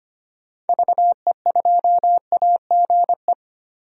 Morse code, heard as a keyed tone, about 25 wpm, 700 Hz. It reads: VE2AGE
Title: DXCC-100 Call Signs - 6 Characters - Repeated 3x - 25wpm